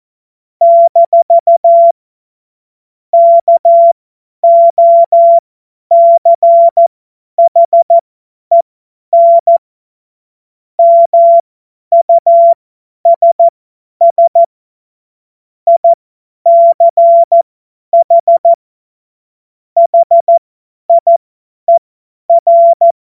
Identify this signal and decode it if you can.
Morse code, heard as a keyed tone, about 14 wpm, spaced as for 10 wpm, 680 Hz. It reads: - KOCHEN MUSS ICH HIER
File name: Morsekurs / Witze / Witz 04. Juli 2023 50 F70